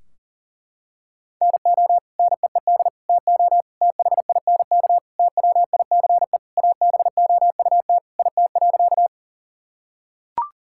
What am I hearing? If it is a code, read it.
Morse code, heard as a keyed tone, about 37 wpm, 700 Hz. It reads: NO NEED TO THINK TWICE ABOUT IT.
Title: Sentences from Top 1600 Words 40wpm